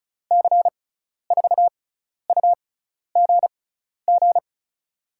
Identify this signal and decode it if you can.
Morse code, heard as a keyed tone, about 36 wpm, spaced as for 12 wpm, 690 Hz. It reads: C4UGG
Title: USA Call Signs - 6 Characters - 6x Character Spacing - 35wpm